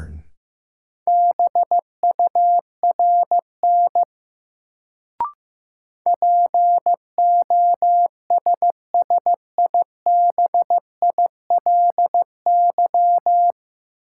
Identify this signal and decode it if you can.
Morse code, heard as a keyed tone, about 15 wpm, 700 Hz. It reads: BURN POSSIBILY
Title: Sentences from Top 1300 Words - New Word Drill - 15wpm